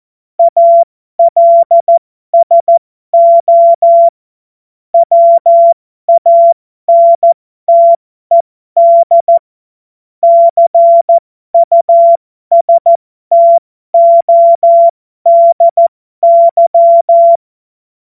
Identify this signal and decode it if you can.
Morse code, heard as a keyed tone, about 14 wpm, spaced as for 12 wpm, 680 Hz. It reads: ALSO WANTED CUSTODY